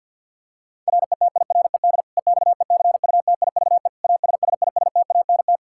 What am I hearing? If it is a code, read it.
Morse code, heard as a keyed tone, about 51 wpm, 700 Hz. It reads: RETIRED EXECUTIVE ASSISTANT
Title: Practice QSO - Exchange 3 - More Info - 50wpm